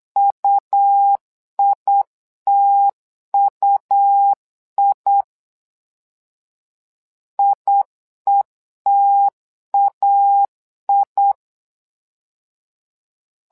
Morse code, heard as a keyed tone, about 8 wpm, 800 Hz. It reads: UITUI IETAI
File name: LEZIONI / 2 UHDN / 2 U 8wpm 2028